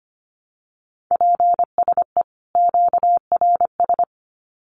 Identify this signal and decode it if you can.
Morse code, heard as a keyed tone, about 25 wpm, 700 Hz. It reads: PSE QRS
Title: Common QSO Phrases 25wpm